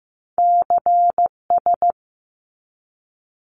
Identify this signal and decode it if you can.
Morse code, heard as a keyed tone, about 15 wpm, 700 Hz. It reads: CS
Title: Abbreviations 15wpm